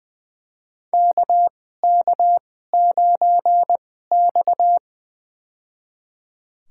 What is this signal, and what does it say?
Morse code, heard as a keyed tone, about 21 wpm, spaced as for 15 wpm, 700 Hz. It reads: KK9X
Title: USA Call Signs - 4 Characters - 2x Character Spacing - 20wpm